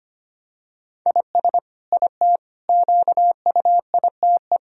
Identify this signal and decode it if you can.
Morse code, heard as a keyed tone, about 25 wpm, 700 Hz. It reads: IS IT QUITE
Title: Sentences from Top 1000 Words - Review Part 3 - 25wpm